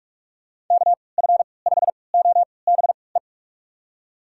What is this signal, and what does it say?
Morse code, heard as a keyed tone, about 46 wpm, spaced as for 25 wpm, 700 Hz. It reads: KF5OBE
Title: USA Call Signs - 6 Characters - 3x Character Spacing - 45wpm